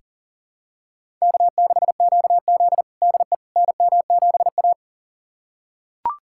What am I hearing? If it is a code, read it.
Morse code, heard as a keyed tone, about 40 wpm, 700 Hz. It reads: K6QZ DE NM7A
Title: USA Callsign DE Callsign - 4 Characters - 40wpm